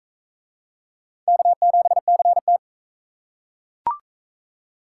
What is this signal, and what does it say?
Morse code, heard as a keyed tone, about 37 wpm, 700 Hz. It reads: K7CT